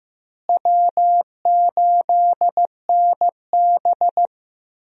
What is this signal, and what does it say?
Morse code, heard as a keyed tone, about 15 wpm, 700 Hz. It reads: W8NB
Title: USA Call Signs - 4 Characters - Repeated 2x - 15wpm